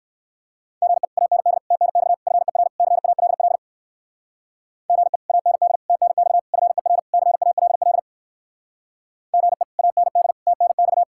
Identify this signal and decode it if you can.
Morse code, heard as a keyed tone, about 57 wpm, 700 Hz. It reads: GE AND TNX FER CALL GE AND TNX FER CALL GE AND TNX